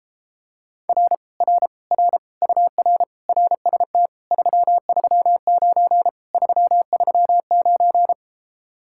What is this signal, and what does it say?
Morse code, heard as a keyed tone, about 33 wpm, 700 Hz. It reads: R R R UR RST 339 339